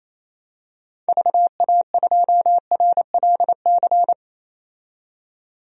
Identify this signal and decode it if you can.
Morse code, heard as a keyed tone, about 27 wpm, 700 Hz. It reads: VA2RLC
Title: DXCC-100 Call Signs - 6 Characters - 28wpm